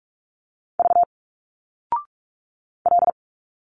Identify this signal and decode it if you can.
Morse code, heard as a keyed tone, about 46 wpm, 700 Hz. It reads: V L
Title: Single Letters 45wpm